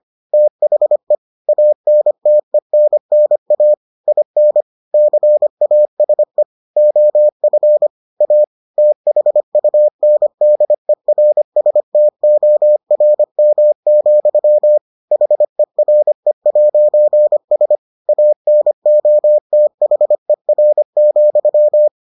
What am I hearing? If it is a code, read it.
Morse code, heard as a keyed tone, about 25 wpm, 600 Hz. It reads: THE ANTENNA IN CASE OF A THUNDERSTORM, HERE'S ANOTHER,